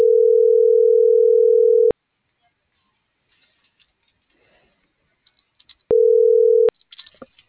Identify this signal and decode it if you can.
Ringback tone (North America), 440 + 480 Hz.